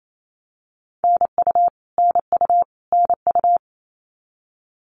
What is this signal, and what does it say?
Morse code, heard as a keyed tone, about 28 wpm, 700 Hz. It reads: NU NU NU
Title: Canadian Provinces and Territories - Repeated 3x - 28wpm